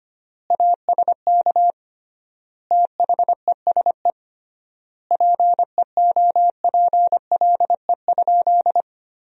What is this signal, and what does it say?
Morse code, heard as a keyed tone, about 25 wpm, 700 Hz. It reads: ASK THESE PEOPLE?
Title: 3x Word-Spacing - Sentences from Top 200 Words - 25wpm